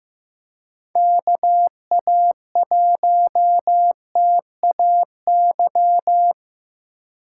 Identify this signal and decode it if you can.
Morse code, heard as a keyed tone, about 18 wpm, 700 Hz. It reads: KA1TAY
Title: Speed Racing - USA 2x3 Call Signs 15wpm